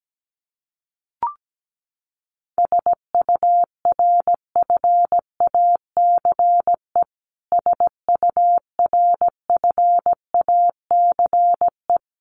Morse code, heard as a keyed tone, about 17 wpm, 700 Hz. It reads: SURFACE SURFACE
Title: Top 500 Words Repeated 17wpm